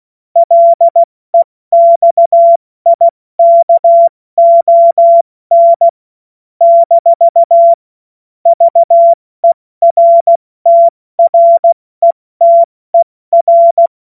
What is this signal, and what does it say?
Morse code, heard as a keyed tone, about 16 wpm, spaced as for 14 wpm, 680 Hz. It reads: LEXIKON - VERTRETER
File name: Morsekurs / Witze / Witz 09 Januar 2024 70 F80